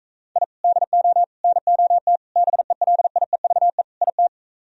Morse code, heard as a keyed tone, about 45 wpm, 700 Hz. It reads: I DO NOT BELIEVE IT